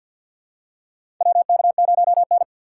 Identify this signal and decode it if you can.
Morse code, heard as a keyed tone, about 51 wpm, 700 Hz. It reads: WK0N
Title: USA Call Signs - 4 Characters - 3x Character Spacing - 50wpm